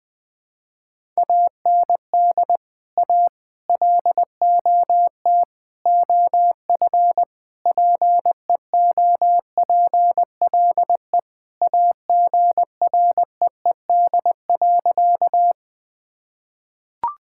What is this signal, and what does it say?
Morse code, heard as a keyed tone, about 21 wpm, 700 Hz. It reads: AND A LOT OF PEOPLE AGREED.
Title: Sentences with 5-Word Phrases - Spoken Beginning - 20wpm